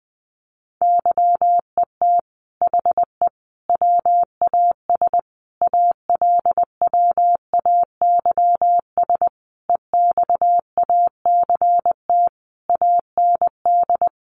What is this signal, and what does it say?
Morse code, heard as a keyed tone, about 19 wpm, 700 Hz. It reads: YET HE WAS ALWAYS EXACT AND
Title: Sentences from Top 1300 Words 20wpm